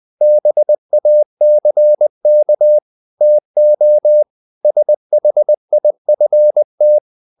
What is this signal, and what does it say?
Morse code, heard as a keyed tone, about 20 wpm, 600 Hz. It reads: BACK TO SHIFT
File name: the old man and the sea / the old man and the sea part 1 20wpm